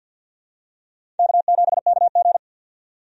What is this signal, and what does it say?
Morse code, heard as a keyed tone, about 48 wpm, 700 Hz. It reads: K7KG
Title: USA Call Signs - 4 Characters - 50wpm